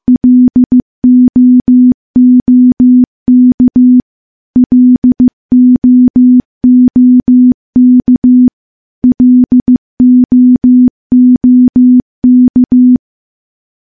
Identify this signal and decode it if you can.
Morse code, heard as a keyed tone, about 15 wpm, 250 Hz. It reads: LOOK LOOK LOOK
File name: morse-training-15WPM